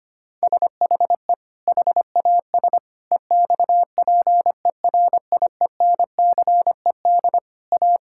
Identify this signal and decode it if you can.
Morse code, heard as a keyed tone, about 25 wpm, 700 Hz. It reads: SHE HAS EXPERIENCED A
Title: Sentences from Top 1600 Words 25wpm